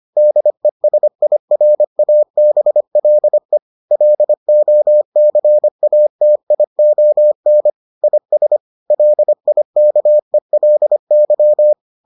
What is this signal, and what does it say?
Morse code, heard as a keyed tone, about 25 wpm, 600 Hz. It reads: DESIRABLE LOCATION IS LIKELY